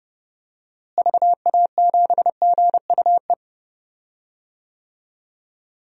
Morse code, heard as a keyed tone, about 32 wpm, 700 Hz. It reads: VA7GUE
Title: DXCC-100 Call Signs - 6 Characters - Repeated 3x - 30wpm